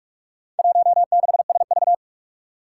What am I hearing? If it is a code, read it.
Morse code, heard as a keyed tone, about 43 wpm, 700 Hz. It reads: 16SV
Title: Four Challenging Letter-Number Combinations 45wpm